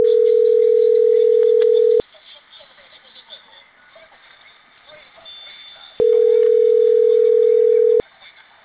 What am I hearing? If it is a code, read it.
Ringback tone (North America), 440 + 480 Hz.